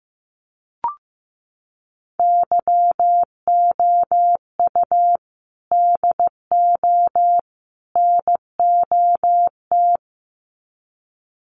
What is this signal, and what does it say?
Morse code, heard as a keyed tone, about 15 wpm, 700 Hz. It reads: YOU DO NOT
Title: Sets of 3 Words 15wpm